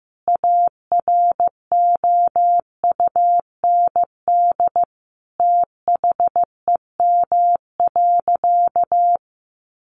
Morse code, heard as a keyed tone, about 15 wpm, 700 Hz. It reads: AROUND THEM.
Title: Sentences from Top 600 Words 15wpm